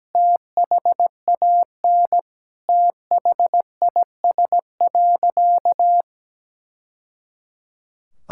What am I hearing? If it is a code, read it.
Morse code, heard as a keyed tone, about 17 wpm, 700 Hz. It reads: THAN THIS.
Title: Sentences from Top 1000 Words - Review Part 6 - 17wpm